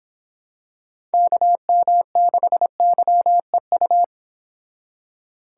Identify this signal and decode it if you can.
Morse code, heard as a keyed tone, about 25 wpm, 700 Hz. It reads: KM6YEU
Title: Speed Racing - USA Call Signs 22wpm